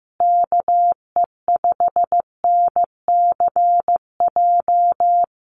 Morse code, heard as a keyed tone, about 15 wpm, 700 Hz. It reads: KE5NCJ
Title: USA Call Signs Repeated 3x 15wpm